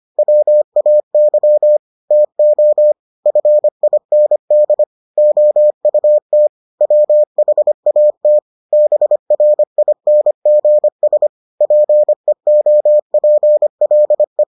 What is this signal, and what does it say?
Morse code, heard as a keyed tone, about 25 wpm, 600 Hz. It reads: WAY TO FIND OUT WHAT BRINGS PEOPLE